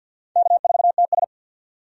Morse code, heard as a keyed tone, about 47 wpm, 700 Hz. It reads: K4TS
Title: USA Call Signs - 4 Characters - 50wpm